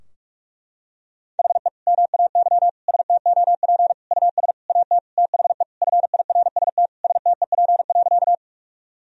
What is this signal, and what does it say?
Morse code, heard as a keyed tone, about 48 wpm, 700 Hz. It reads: HE MAY STOP US AT THE FIRST STEP.
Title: Sentences from Top 1000 Words - Review Part 2 - 45wpm